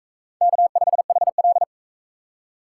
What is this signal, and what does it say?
Morse code, heard as a keyed tone, about 44 wpm, 700 Hz. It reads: K5HL